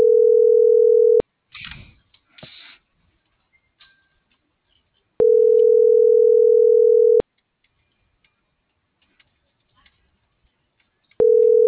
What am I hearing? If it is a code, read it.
Ringback tone (North America), 440 + 480 Hz.